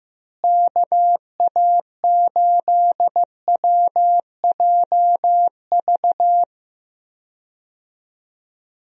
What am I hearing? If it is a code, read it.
Morse code, heard as a keyed tone, about 15 wpm, 700 Hz. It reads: KA8WJV
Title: USA Call Signs - 6 Characters - Repeated 2x - 15wpm